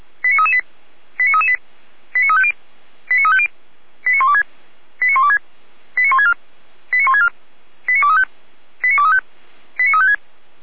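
Five-tone selective call (ZVEI), heard as quick runs of five tones, tones of about 70 ms. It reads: 89338 89338 89350 89350 89126 89126 89164 89164 89235 89235 89356
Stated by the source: Five-tone ZVEI selective call received on SDR.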